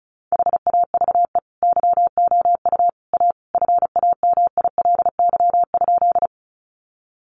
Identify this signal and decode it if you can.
Morse code, heard as a keyed tone, about 34 wpm, 700 Hz. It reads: HAVE YOU A FAMILY?
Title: Sentences from Top 400 Words 35wpm